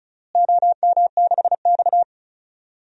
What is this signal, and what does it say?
Morse code, heard as a keyed tone, about 34 wpm, 700 Hz. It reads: OM6X
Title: DXCC-100 Call Signs - 4 Characters - Repeated 2x - 35wpm